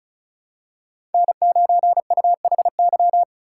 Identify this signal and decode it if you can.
Morse code, heard as a keyed tone, about 36 wpm, 700 Hz. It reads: N9UHY
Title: USA Call Signs - 5 Characters - 4x Character Spacing - 35wpm